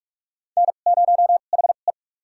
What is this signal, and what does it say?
Morse code, heard as a keyed tone, about 43 wpm, spaced as for 32 wpm, 700 Hz. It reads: N0HE
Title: USA Call Signs - 4 Characters - 2x Character Spacing - 45wpm